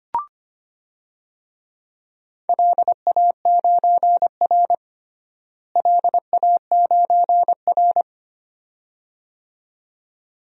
Morse code, heard as a keyed tone, about 25 wpm, 700 Hz. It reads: LA9R LA9R
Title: DXCC-100 Call Signs - 4 Characters - Repeated 2x - 25wpm